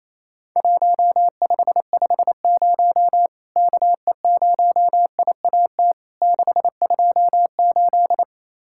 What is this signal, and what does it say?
Morse code, heard as a keyed tone, about 28 wpm, 700 Hz. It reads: 1550 KE0IAT 628
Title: Worked All Europe DX Contest - QTC Exchange - 28wpm